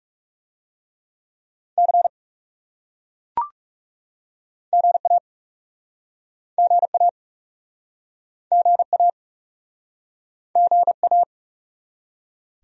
Morse code, heard as a keyed tone, about 40 wpm, 700 Hz. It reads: C GA GA GA GA